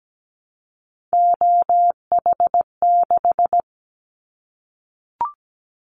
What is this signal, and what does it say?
Morse code, heard as a keyed tone, about 17 wpm, 700 Hz. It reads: OH6
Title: Callsign Prefixes for Top 25 DXCC - 17wpm